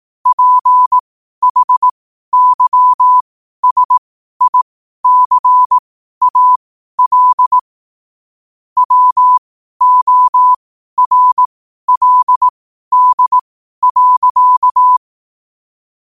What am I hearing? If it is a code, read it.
Morse code, heard as a keyed tone, about 18 wpm, spaced as for 12 wpm, 1000 Hz. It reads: PHYSICAL WORLD.
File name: qotdfiles / 12wpm